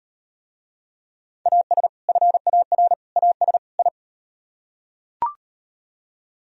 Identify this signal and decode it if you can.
Morse code, heard as a keyed tone, about 36 wpm, 700 Hz. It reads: AS FAR AS I